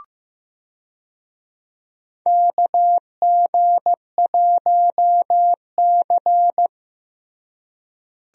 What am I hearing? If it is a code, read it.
Morse code, heard as a keyed tone, about 15 wpm, 700 Hz. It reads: KG1C
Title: USA Call Signs - 4 Characters - Repeated 3x - 15wpm